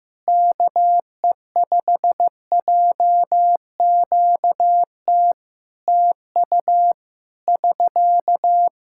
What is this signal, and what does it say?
Morse code, heard as a keyed tone, about 15 wpm, 700 Hz. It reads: KE5JQT TU <SK>
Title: Practice QSO - Exchange 4 - Ending - 15wpm